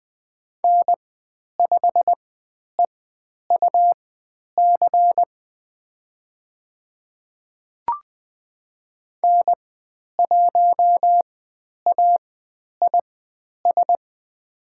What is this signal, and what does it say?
Morse code, heard as a keyed tone, about 20 wpm, spaced as for 10 wpm, 700 Hz. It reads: N5EUC N1AIS
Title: 109 Lesson 18g - Letter m - Call Signs 20wpm